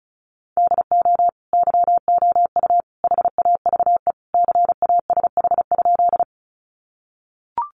Morse code, heard as a keyed tone, about 34 wpm, 700 Hz. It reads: DO YOU HAVE CASH?
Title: Sentences from Top 700 Words 35wpm